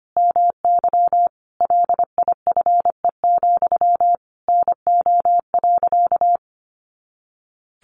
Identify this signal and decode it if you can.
Morse code, heard as a keyed tone, about 25 wpm, 700 Hz. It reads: MY LIFE, NO.
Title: Sentences from Top 300 Words 25wpm